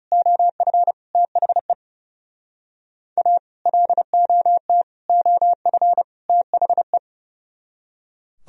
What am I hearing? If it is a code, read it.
Morse code, heard as a keyed tone, about 37 wpm, 700 Hz. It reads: OF THE A LOT OF THE
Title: Speed-Racing - Sets of 4 Words 30wpm